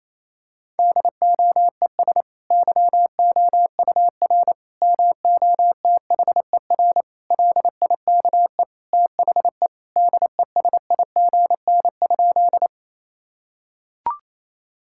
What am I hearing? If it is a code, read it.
Morse code, heard as a keyed tone, about 28 wpm, 700 Hz. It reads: DOES YOUR MOTHER LIKE THE DESIGN?
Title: Sentences from Top 1000 Words - Review Part 5 - 28wpm